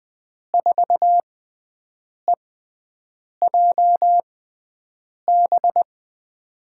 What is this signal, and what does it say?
Morse code, heard as a keyed tone, about 20 wpm, spaced as for 7 wpm, 700 Hz. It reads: 4EJB
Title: USA Call Signs - 6 Characters - 6x Character Spacing - 20wpm